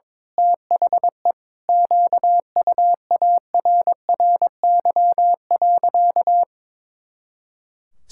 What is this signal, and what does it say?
Morse code, heard as a keyed tone, about 21 wpm, 700 Hz. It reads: THE QUARRY.
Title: Pangram Challenge v2 - 22wpm